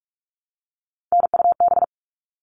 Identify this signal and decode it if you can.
Morse code, heard as a keyed tone, about 44 wpm, 700 Hz. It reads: NUB
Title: All Cut Numbers - 3-Digit 45wpm